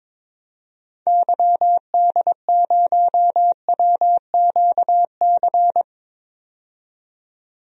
Morse code, heard as a keyed tone, about 21 wpm, 700 Hz. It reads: YD0WQC
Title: DXCC-100 Call Signs - 6 Characters - 22wpm